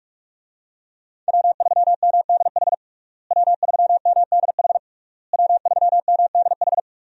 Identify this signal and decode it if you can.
Morse code, heard as a keyed tone, about 46 wpm, 700 Hz. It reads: W3MDH W3MDH W3MDH
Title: USA Call Signs - 5 Characters - Repeated 3x - 45wpm